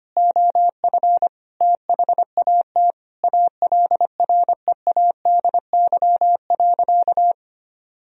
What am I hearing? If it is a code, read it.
Morse code, heard as a keyed tone, about 25 wpm, 700 Hz. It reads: OF THAT ALREADY.
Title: Sentences from Top 1000 Words - Review Part 3 - 25wpm